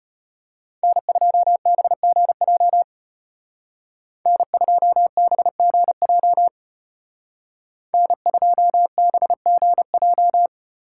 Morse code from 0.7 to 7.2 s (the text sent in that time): N2BGJ N2BGJ